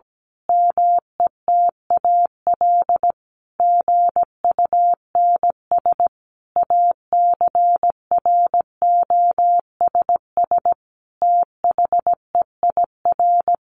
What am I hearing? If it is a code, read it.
Morse code, heard as a keyed tone, about 17 wpm, 700 Hz. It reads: METAL GUNS ACROSS THEIR
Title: A Sound Of Thunder 17wpm